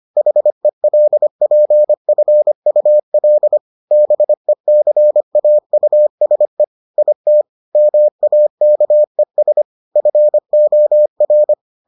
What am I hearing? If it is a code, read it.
Morse code, heard as a keyed tone, about 25 wpm, 600 Hz. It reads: HELPFUL BECAUSE IT MAKES FOR